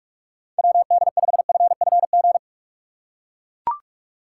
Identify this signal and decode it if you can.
Morse code, heard as a keyed tone, about 41 wpm, 700 Hz. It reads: WD5FFG
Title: USA Call Signs - 6 Characters - 2x Character Spacing - 45wpm